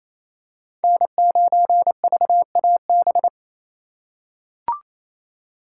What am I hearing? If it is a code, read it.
Morse code, heard as a keyed tone, about 28 wpm, 700 Hz. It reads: N9VAB
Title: USA Call Signs - 5 Characters - 2x Character Spacing - 28wpm